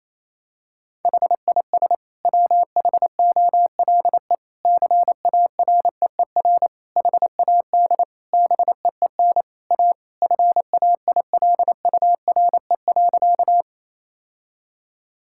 Morse code, heard as a keyed tone, about 28 wpm, 700 Hz. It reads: HIS WHOLE CAREER HAD BEEN A FAILURE.
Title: Sentences from Top 1000 Words - Review Part 6 - 28wpm